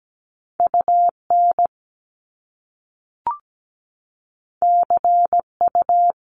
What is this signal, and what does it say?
Morse code, heard as a keyed tone, about 17 wpm, 700 Hz. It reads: UN CU
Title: Top Two Letter Combinations in Words 17wpm